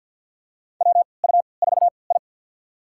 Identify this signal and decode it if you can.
Morse code, heard as a keyed tone, about 50 wpm, spaced as for 28 wpm, 700 Hz. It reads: WU4I
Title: USA Call Signs - 3x Character Spacing - 50wpm